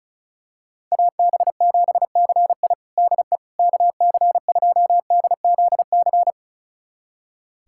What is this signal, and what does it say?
Morse code, heard as a keyed tone, about 36 wpm, 700 Hz. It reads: AB7CI DE KC2DZC
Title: USA Callsign DE Callsign 35wpm